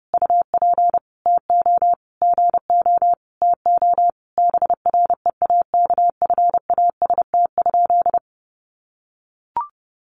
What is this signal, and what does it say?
Morse code, heard as a keyed tone, about 30 wpm, 700 Hz. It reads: UP TO GO TO BREAKFAST?
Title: Sentences from Top 1500 Words 30wpm